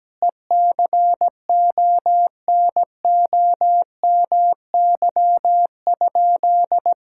Morse code, heard as a keyed tone, about 17 wpm, 700 Hz. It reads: ECONOMY?
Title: Sentences from Top 1000 Words - Review Part 6 - 17wpm